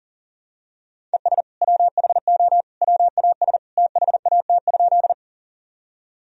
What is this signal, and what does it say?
Morse code, heard as a keyed tone, about 40 wpm, 700 Hz. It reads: ES WHO WAS THAT?
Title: Sentences from Top 100 Words and Abbreviations 40wpm